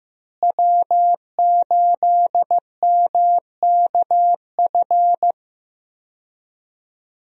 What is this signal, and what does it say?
Morse code, heard as a keyed tone, about 16 wpm, 700 Hz. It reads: W8MKF
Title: USA Call Signs - 5 Characters - Repeated 2x - 15wpm